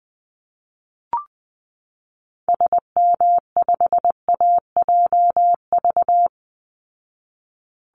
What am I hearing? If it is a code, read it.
Morse code, heard as a keyed tone, about 20 wpm, 700 Hz. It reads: SM5AJV
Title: Call Signs Repeated 3x - 20wpm